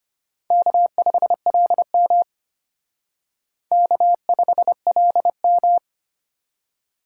Morse code, heard as a keyed tone, about 31 wpm, 700 Hz. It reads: K5LM K5LM
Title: Speed Racing - USA 1x2 Call Signs 25wpm